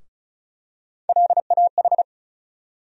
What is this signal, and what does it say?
Morse code, heard as a keyed tone, about 36 wpm, 700 Hz. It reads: LAH
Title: USA Call Sign Postfixes - 3 Letters 35wpm